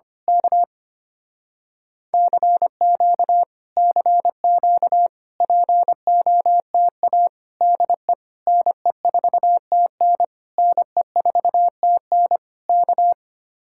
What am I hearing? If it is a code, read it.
Morse code, heard as a keyed tone, about 24 wpm, 700 Hz. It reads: K CQ CQ POTA DE NE4TN NE4TN K
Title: Speed-Racing Common QSO Phrases - 25wpm